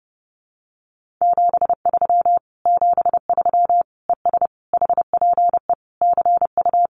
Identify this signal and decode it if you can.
Morse code, heard as a keyed tone, about 28 wpm, 700 Hz. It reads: 73 73 ES HPE CU
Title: Common QSO Phrases Encore 30wpm